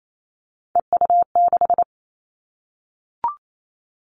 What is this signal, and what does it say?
Morse code, heard as a keyed tone, about 28 wpm, 700 Hz. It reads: EU6
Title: Callsign Prefixes for Top 100 DXCC - 28wpm